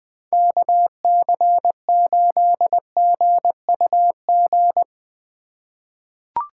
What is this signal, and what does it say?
Morse code, heard as a keyed tone, about 20 wpm, 700 Hz. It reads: KC8GUG
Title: USA Call Signs - 6 Characters - 3x Character Spacing - 20wpm